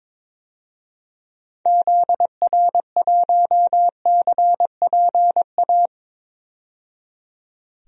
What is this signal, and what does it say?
Morse code, heard as a keyed tone, about 22 wpm, 700 Hz. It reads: ZR1CPA
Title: DXCC-100 Call Signs - 6 Characters - 22wpm